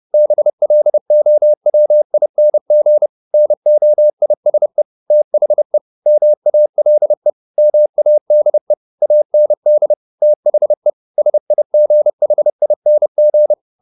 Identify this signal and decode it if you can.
Morse code, heard as a keyed tone, about 30 wpm, 600 Hz. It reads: BLOWING NOISE THE MALE MADE AND THE SIGHING
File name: the old man and the sea / the old man and the sea 1 30wpm